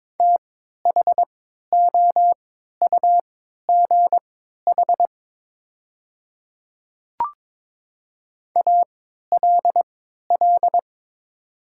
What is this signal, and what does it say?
Morse code, heard as a keyed tone, about 22 wpm, spaced as for 12 wpm, 700 Hz. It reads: THOUGH ALL
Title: 3x Character Spacing - Top 500 Words - 22wpm